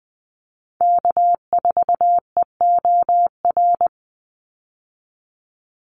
Morse code, heard as a keyed tone, about 20 wpm, 700 Hz. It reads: K4EOR
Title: Call Signs - 20wpm